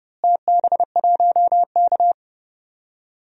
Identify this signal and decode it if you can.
Morse code, heard as a keyed tone, about 30 wpm, 700 Hz. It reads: TB1K
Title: DXCC-100 Call Signs - 4 Characters - Repeated 2x - 30wpm